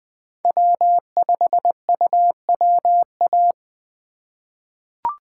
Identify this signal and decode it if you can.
Morse code, heard as a keyed tone, about 21 wpm, 700 Hz. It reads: W5UWA
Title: USA Call Signs - 5 Characters - 3x Character Spacing - 20wpm